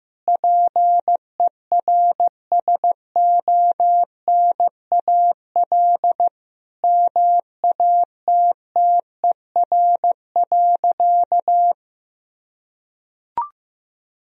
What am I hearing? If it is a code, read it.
Morse code, heard as a keyed tone, about 15 wpm, 700 Hz. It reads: PERSONAL MATTER.
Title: Sentences from Top 1000 Words - Review Part 4 - 15wpm